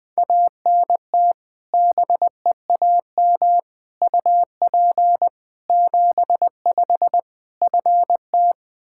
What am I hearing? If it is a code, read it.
Morse code, heard as a keyed tone, about 20 wpm, 700 Hz. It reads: ANT BEAM UP 75 FT
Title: Practice QSO - Exchange 2 - Info - 20wpm